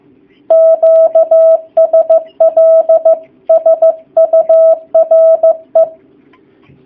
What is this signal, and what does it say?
Morse code, heard as a keyed tone, about 15 wpm, 650 Hz. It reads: QSL SURE
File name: jh8cbh / 2019cw / sp15- w